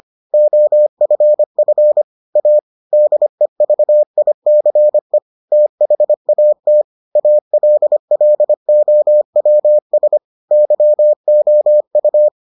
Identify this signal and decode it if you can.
Morse code, heard as a keyed tone, about 25 wpm, 600 Hz. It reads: OFF A DEVICE THAT ALLOWS YOU